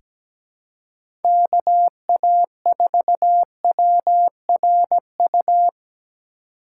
Morse code, heard as a keyed tone, about 17 wpm, 700 Hz. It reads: KA4WRU
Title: USA Call Signs - 6 Characters - 2x Character Spacing - 17wpm